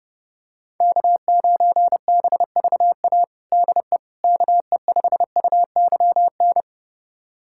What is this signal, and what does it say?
Morse code, heard as a keyed tone, about 30 wpm, 700 Hz. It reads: K9BVA DE KE5UYN
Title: USA Callsign DE Callsign 30wpm